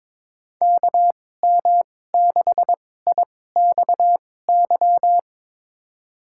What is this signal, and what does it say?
Morse code, heard as a keyed tone, about 21 wpm, spaced as for 16 wpm, 700 Hz. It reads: KM6IXY
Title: USA Call Signs - 2x Character Spacing - 22wpm